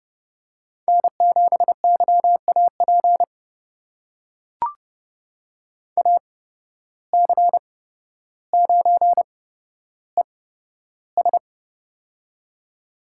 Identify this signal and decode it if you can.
Morse code, heard as a keyed tone, about 30 wpm, spaced as for 13 wpm, 700 Hz. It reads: N7YAP AC9ES
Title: USA Call Signs - 5 Characters - 8x Character Spacing - 30wpm